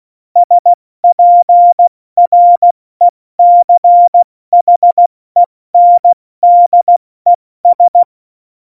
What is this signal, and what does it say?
Morse code, heard as a keyed tone, about 16 wpm, spaced as for 14 wpm, 700 Hz. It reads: SPRECHENDES
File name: Morsekurs / Witze / Witz 02. Oktober 2018, 70 F80